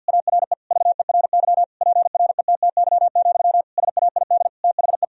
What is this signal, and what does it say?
Morse code, heard as a keyed tone, about 50 wpm, 700 Hz. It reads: ARE VERY PRETTY, SAID THE
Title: Sentences from Top 1000 Words - Review Part 4 - 50wpm